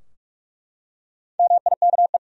Morse code, heard as a keyed tone, about 41 wpm, 700 Hz. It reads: MIKE
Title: USA Common Names - Max 4 Letters - 45wpm